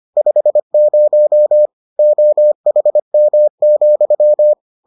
Morse code, heard as a keyed tone, about 25 wpm, 600 Hz. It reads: 50 OHM,